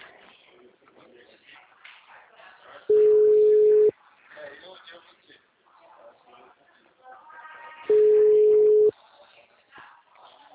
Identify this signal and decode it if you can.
Ringback tone (Europe, 425 Hz), with speech.